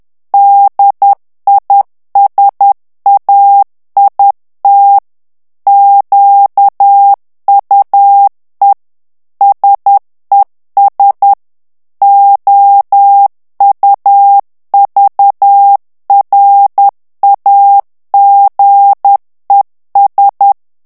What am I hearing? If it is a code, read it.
Morse code, heard as a keyed tone, about 10 wpm, 800 Hz. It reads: DISAIT QUE SES OUVRAGES
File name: FRANCAIS TEXTE CLAIR 2 VITESSE 10 MOTS MINUTES